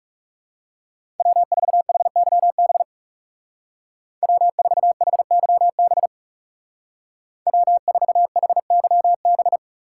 Morse code, heard as a keyed tone, about 34 wpm, 700 Hz. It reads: W4HYB W4HYB W4HYB
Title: Speed Racing - USA 1x3 Call Signs 30wpm